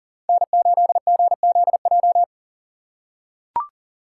Morse code, heard as a keyed tone, about 40 wpm, 700 Hz. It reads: N8GZJ
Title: USA Call Signs - 5 Characters - 3x Character Spacing - 40wpm